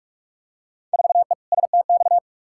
Morse code, heard as a keyed tone, about 42 wpm, 700 Hz. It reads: 4E STX
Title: Field Day Exchanges Encore - 45wpm